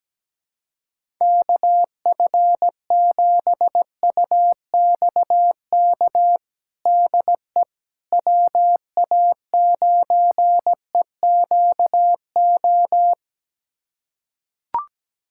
Morse code, heard as a keyed tone, about 17 wpm, 700 Hz. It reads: KF7UXK DE WA9EQO
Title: USA Callsign DE Callsign 17wpm